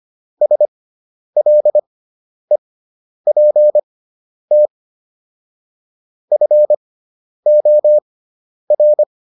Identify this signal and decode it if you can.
Morse code, heard as a keyed tone, about 25 wpm, spaced as for 10 wpm, 600 Hz. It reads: SLEPT FOR